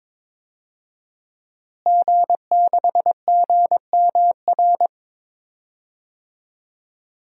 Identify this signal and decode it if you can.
Morse code, heard as a keyed tone, about 22 wpm, 700 Hz. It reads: G6GMR
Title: DXCC-100 Call Signs - 5 Characters - Repeated 2x - 22wpm